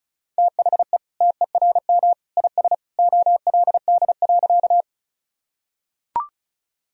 Morse code, heard as a keyed tone, about 34 wpm, 700 Hz. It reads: THE TERM IS OLD.
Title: Sentences from Top 1000 Words - Review Part 4 - 35wpm